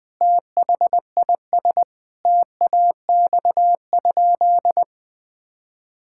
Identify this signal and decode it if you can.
Morse code, heard as a keyed tone, about 20 wpm, 700 Hz. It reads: THIS TAX?
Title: Sentences from Top 1000 Words - Review Part 5 - 20wpm